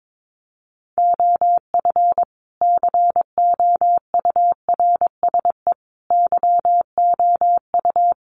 Morse code, heard as a keyed tone, about 22 wpm, 700 Hz. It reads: OF COURSE YOU
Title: Sentences from Top 1200 Words 22wpm